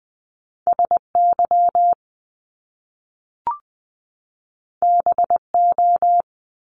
Morse code, heard as a keyed tone, about 20 wpm, 700 Hz. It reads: S Y B O
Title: Top Two Letter Combinations in Words 20wpm